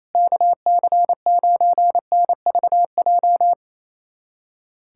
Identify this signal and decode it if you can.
Morse code, heard as a keyed tone, about 27 wpm, 700 Hz. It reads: KC9NVJ
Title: USA Callsign DE Callsign - 6 Characters - 28wpm